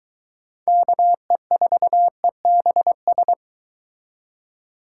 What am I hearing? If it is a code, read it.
Morse code, heard as a keyed tone, about 23 wpm, 700 Hz. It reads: KE4EBS